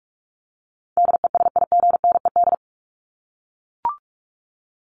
Morse code, heard as a keyed tone, about 44 wpm, 700 Hz. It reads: DESIGNED
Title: ED Suffix - Top 100 Words with Base Words 45wpm